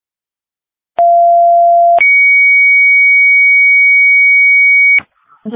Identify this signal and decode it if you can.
Two-tone page, heard as a one-second tone, then a three-second tone, about 690 then 2250 Hz, with speech. It is heard from 1.0 to 5.0 s.